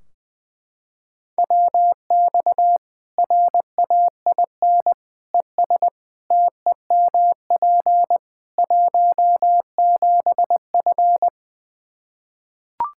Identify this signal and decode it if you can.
Morse code, heard as a keyed tone, about 20 wpm, 700 Hz. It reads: WX RAIN ES TEMP 17F
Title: Practice QSO - Exchange 2 - Info - 20wpm